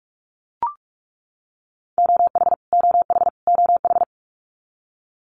Morse code, heard as a keyed tone, about 44 wpm, 700 Hz. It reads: OH OH OH
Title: Top 500 Words Repeated 45wpm